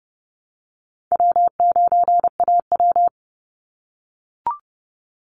Morse code, heard as a keyed tone, about 30 wpm, 700 Hz. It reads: W9AW
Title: Call Signs - 30wpm